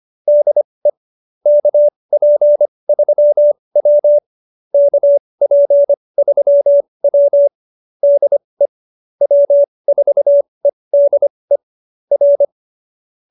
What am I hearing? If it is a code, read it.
Morse code, heard as a keyed tone, about 25 wpm, spaced as for 20 wpm, 590 Hz. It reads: DE KP3W KP3W DE W4EDE R